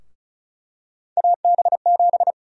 Morse code, heard as a keyed tone, about 36 wpm, 700 Hz. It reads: AB7
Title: USA Call Sign Prefixes - Repeated 2x - 35wpm